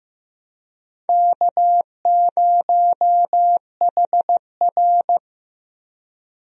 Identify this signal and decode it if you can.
Morse code, heard as a keyed tone, about 15 wpm, 700 Hz. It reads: K0HR
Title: USA Call Signs - 4 Characters - Repeated 3x - 15wpm